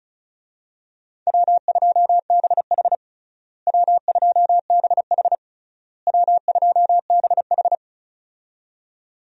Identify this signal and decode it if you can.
Morse code, heard as a keyed tone, about 34 wpm, 700 Hz. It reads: W2BH W2BH W2BH
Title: USA Call Signs - 4 Characters - Repeated 3x - 35wpm